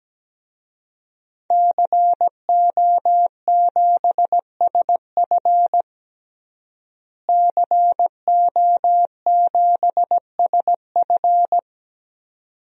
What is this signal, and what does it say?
Morse code, heard as a keyed tone, about 17 wpm, 700 Hz. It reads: CO7SF CO7SF
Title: DXCC-100 Call Signs - 5 Characters - Repeated 2x - 17wpm